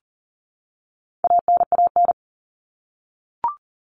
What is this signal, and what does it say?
Morse code, heard as a keyed tone, about 40 wpm, 700 Hz. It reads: ANAN
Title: All Cut Numbers - 4-Digit 40wpm